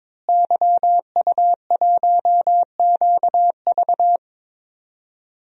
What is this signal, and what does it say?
Morse code, heard as a keyed tone, about 21 wpm, 700 Hz. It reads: YU1QV
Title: DXCC-100 Call Signs - 5 Characters - Repeated 2x - 22wpm